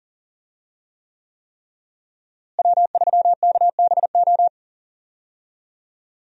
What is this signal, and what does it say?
Morse code, heard as a keyed tone, about 39 wpm, 700 Hz. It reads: W3KBO
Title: USA Call Signs - 5 Characters - Repeated 3x - 40wpm